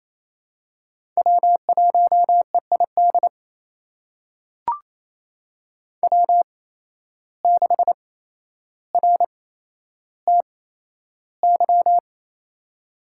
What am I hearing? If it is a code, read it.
Morse code, heard as a keyed tone, about 28 wpm, spaced as for 12 wpm, 700 Hz. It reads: W1EID W6RTY
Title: USA Call Signs - 5 Characters - 8x Character Spacing - 28wpm